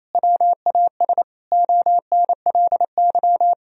Morse code, heard as a keyed tone, about 28 wpm, 700 Hz. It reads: WAS ONLY
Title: Sentences from Top 1000 Words - Review Part 2 - 28wpm